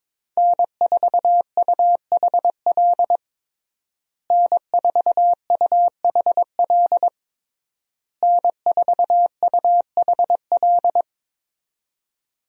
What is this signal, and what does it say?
Morse code, heard as a keyed tone, about 22 wpm, 700 Hz. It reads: N4UHL N4UHL N4UHL
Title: USA Call Signs - 5 Characters - Repeated 3x - 22wpm